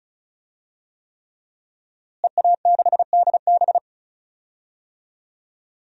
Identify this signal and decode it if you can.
Morse code, heard as a keyed tone, about 36 wpm, 700 Hz. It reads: EA6DB
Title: DXCC-100 Call Signs - 5 Characters - Repeated 3x - 35wpm